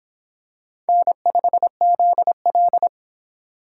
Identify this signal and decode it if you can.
Morse code, heard as a keyed tone, about 25 wpm, 700 Hz. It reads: N5ZL